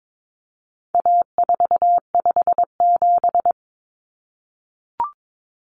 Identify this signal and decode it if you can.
Morse code, heard as a keyed tone, about 22 wpm, 700 Hz. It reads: A457
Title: Common Cut Numbers - 4-Digit 22wpm